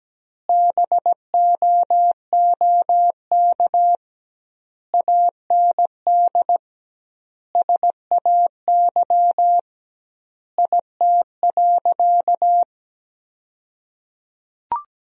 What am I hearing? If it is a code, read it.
Morse code, heard as a keyed tone, about 17 wpm, 700 Hz. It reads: BOOK AND SAY IT.
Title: 2x Word-Spacing - Sentences from Top 400 Words - 17wpm